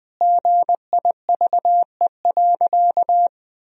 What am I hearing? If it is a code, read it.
Morse code, heard as a keyed tone, about 20 wpm, 700 Hz. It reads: GIVE.
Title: Sentences from Top 1000 Words - Review Part 4 - 20wpm